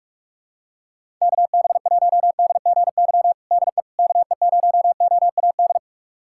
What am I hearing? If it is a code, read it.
Morse code, heard as a keyed tone, about 41 wpm, 700 Hz. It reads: KB1DGY DE KE0OAD
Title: USA Callsign DE Callsign - 6 Characters - 45wpm